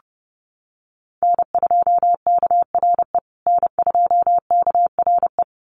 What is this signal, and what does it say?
Morse code, heard as a keyed tone, about 30 wpm, 700 Hz. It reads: N2KRE N2KRE
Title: USA Call Signs Repeated 2x 30wpm